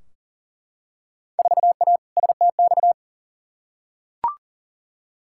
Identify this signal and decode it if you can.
Morse code, heard as a keyed tone, about 40 wpm, 700 Hz. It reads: 4A STX
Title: Field Day Exchanges - 40wpm